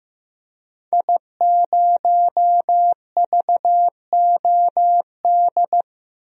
Morse code, heard as a keyed tone, about 15 wpm, 700 Hz. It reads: I0VOD
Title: DXCC-100 Call Signs - 5 Characters - 15wpm